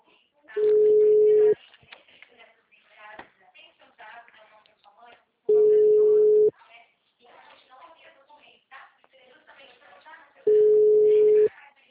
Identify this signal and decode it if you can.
Ringback tone (Europe, 425 Hz), with speech.